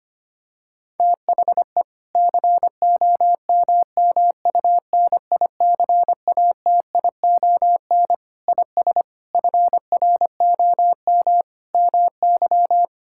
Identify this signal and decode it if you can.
Morse code, heard as a keyed tone, about 25 wpm, 700 Hz. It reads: THE COMMUNICATION IS FROM MY
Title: Sentences from Top 1000 Words - Review Part 6 - 25wpm